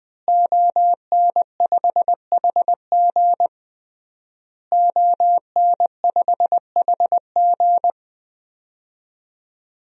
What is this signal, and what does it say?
Morse code, heard as a keyed tone, about 20 wpm, 700 Hz. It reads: ON5HG ON5HG
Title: DXCC-100 Call Signs - 5 Characters - Repeated 3x - 20wpm